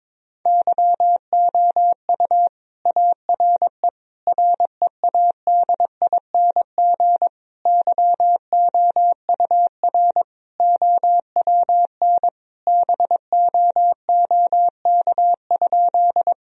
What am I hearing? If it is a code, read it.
Morse code, heard as a keyed tone, about 22 wpm, 700 Hz. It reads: YOU ARE READING YOUR OWN BOOK?
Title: Sentences from Top 1000 Words - Review Part 3 - 22wpm